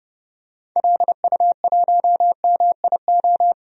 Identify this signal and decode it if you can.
Morse code, heard as a keyed tone, about 31 wpm, 700 Hz. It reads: LU1MIO
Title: DXCC-100 Call Signs - 6 Characters - 30wpm